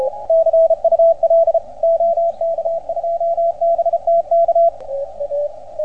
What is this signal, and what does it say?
Morse code, heard as a keyed tone, about 34 wpm, 690 Hz, with radio noise. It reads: CUL OK2BTK